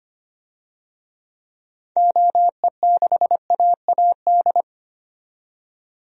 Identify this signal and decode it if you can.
Morse code, heard as a keyed tone, about 25 wpm, 700 Hz. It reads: OE6AAD
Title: DXCC-100 Call Signs - 6 Characters - 25wpm